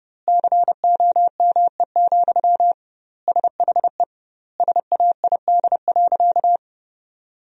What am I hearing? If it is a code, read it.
Morse code, heard as a keyed tone, about 32 wpm, 700 Hz. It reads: COME, SHE SAID.
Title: 2x Word-Spacing - Sentences from Top 100 Words - 30wpm